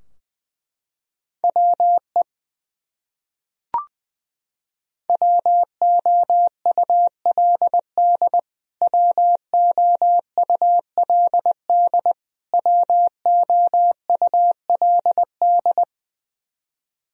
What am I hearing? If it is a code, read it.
Morse code, heard as a keyed tone, about 20 wpm, 700 Hz. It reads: WE WOULD WOULD WOULD
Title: Top 100 Words Repeated 20wpm